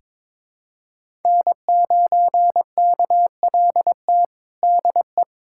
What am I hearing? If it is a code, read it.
Morse code, heard as a keyed tone, about 22 wpm, 700 Hz. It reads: N9KLT DE
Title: USA Callsign DE Callsign - 5 Characters - 22wpm